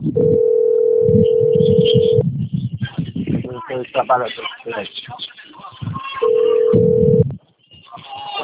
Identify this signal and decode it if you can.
Ringback tone (North America), 440 + 480 Hz, with speech.
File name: out- - - - - .2108